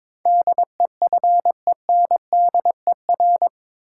Morse code, heard as a keyed tone, about 21 wpm, 700 Hz. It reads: DEFENDER
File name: practice / 22wpm / Eight Letter Words - 22wpm